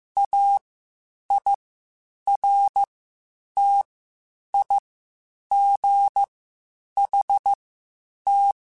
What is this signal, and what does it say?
Morse code, heard as a keyed tone, about 15 wpm, spaced as for 8 wpm, 800 Hz. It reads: AIRTIGHT
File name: Quotes Spaced / Quotes Spaced-mp3s / Airtight sp @13